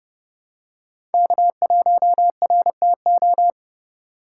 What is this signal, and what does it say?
Morse code, heard as a keyed tone, about 29 wpm, 700 Hz. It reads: K1RTO
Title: USA Call Signs - 5 Characters - 6x Character Spacing - 30wpm